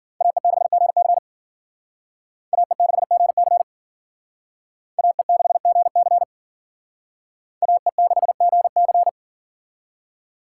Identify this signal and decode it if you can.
Morse code, heard as a keyed tone, about 52 wpm, 700 Hz. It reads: AE6GC AE6GC AE6GC AE6GC